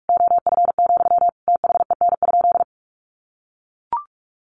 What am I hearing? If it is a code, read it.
Morse code, heard as a keyed tone, about 44 wpm, 700 Hz. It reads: OF, THEN?
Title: Sentences from Top 1500 Words 45wpm